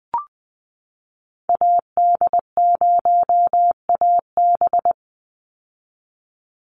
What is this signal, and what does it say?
Morse code, heard as a keyed tone, about 20 wpm, 700 Hz. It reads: AD0AB
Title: Call Signs - 20wpm